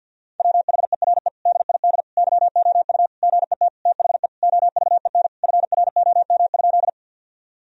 Morse code, heard as a keyed tone, about 50 wpm, 700 Hz. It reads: WHERE DID YOU GET THE OVEN FROM?
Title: Sentences from Top 1000 Words - Review Part 4 - 50wpm